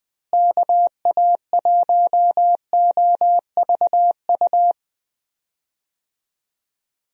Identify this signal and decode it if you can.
Morse code, heard as a keyed tone, about 19 wpm, 700 Hz. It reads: KA1OVU
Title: USA Call Signs - 6 Characters - Repeated 3x - 20wpm